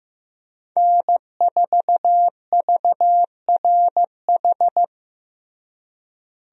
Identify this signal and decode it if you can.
Morse code, heard as a keyed tone, about 15 wpm, 700 Hz. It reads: N4VRH
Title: USA Call Signs - 5 Characters - Repeated 2x - 15wpm